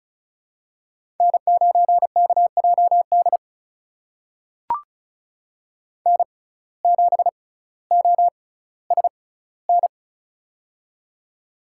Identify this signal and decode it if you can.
Morse code, heard as a keyed tone, about 36 wpm, spaced as for 18 wpm, 700 Hz. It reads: N9KJD N7OSN
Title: USA Call Signs - 5 Characters - 6x Character Spacing - 35wpm